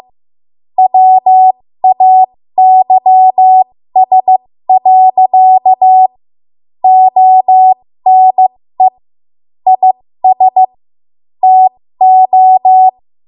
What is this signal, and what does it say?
Morse code, heard as a keyed tone, about 15 wpm, spaced as for 13 wpm, 750 Hz. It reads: WAYS. ONE IS TO
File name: Morse / 13WPM